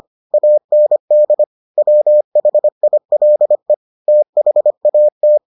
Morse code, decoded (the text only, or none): AND WHILE THAT